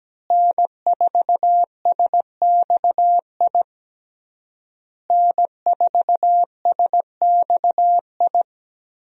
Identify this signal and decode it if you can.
Morse code, heard as a keyed tone, about 17 wpm, 700 Hz. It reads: N4SXI N4SXI
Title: USA Call Signs - 5 Characters - Repeated 3x - 17wpm